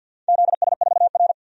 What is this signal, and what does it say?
Morse code, heard as a keyed tone, about 50 wpm, 700 Hz. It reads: ZS4R
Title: DXCC-100 Call Signs - 4 Characters - 50wpm